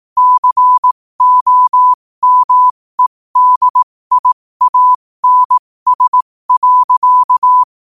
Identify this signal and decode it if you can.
Morse code, heard as a keyed tone, about 18 wpm, spaced as for 15 wpm, 1000 Hz. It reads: COMEDIANS.
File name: qotdfiles / 15wpm